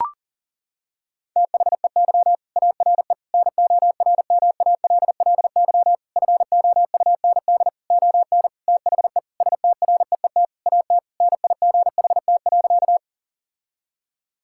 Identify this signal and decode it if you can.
Morse code, heard as a keyed tone, about 40 wpm, 700 Hz. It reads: THEY ARE NORMALLY FOUND ON THE STREET AT NIGHT.
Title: Sentences from Top 1000 Words - Review Part 6 - 40wpm